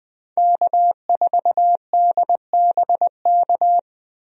Morse code, heard as a keyed tone, about 20 wpm, 700 Hz. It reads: K4DBK
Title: USA Call Signs - 5 Characters - Repeated 2x - 20wpm